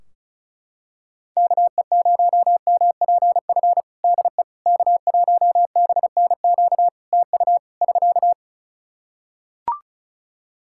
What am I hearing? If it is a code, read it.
Morse code, heard as a keyed tone, about 34 wpm, 700 Hz. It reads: KE0MPF DE K1BNQ TU <SK>
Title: Practice QSO - Exchange 4 - Ending - 35wpm